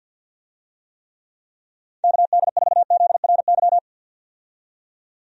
Morse code, heard as a keyed tone, about 45 wpm, 700 Hz. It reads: KD4ZRY
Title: USA Call Signs - 6 Characters - 50wpm